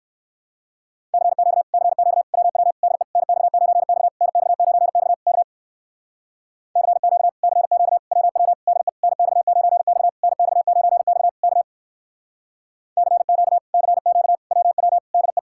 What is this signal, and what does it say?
Morse code, heard as a keyed tone, about 64 wpm, 700 Hz. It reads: CQ CQ WW DE NX9X NX9X K CQ CQ WW DE NX9X NX9X K CQ CQ WW DE